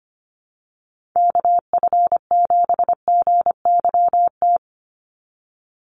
Morse code, heard as a keyed tone, about 25 wpm, 690 Hz. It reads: KF7GYT
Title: USA Call Signs 25wpm